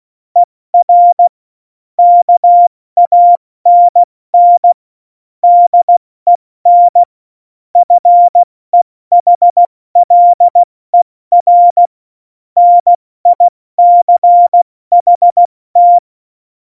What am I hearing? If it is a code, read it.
Morse code, heard as a keyed tone, about 16 wpm, spaced as for 14 wpm, 700 Hz. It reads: ER KANN DEN FEHLER NICHT
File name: Morsekurs / Witze / Witz 29. Januar 2019, 70 F80